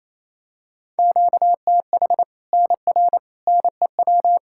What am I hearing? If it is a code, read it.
Morse code, heard as a keyed tone, about 28 wpm, 700 Hz. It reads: QTH NR NEW
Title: Practice QSO - Exchange 1 - Beginning - 28wpm